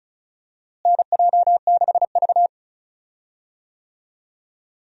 Morse code, heard as a keyed tone, about 36 wpm, 700 Hz. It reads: NJ6V
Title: USA Call Signs - 4 Characters - Repeated 2x - 35wpm